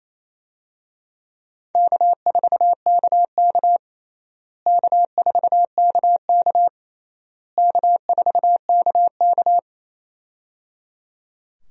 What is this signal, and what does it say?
Morse code, heard as a keyed tone, about 27 wpm, 700 Hz. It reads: K4KK K4KK K4KK
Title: USA Call Signs - 4 Characters - Repeated 3x - 28wpm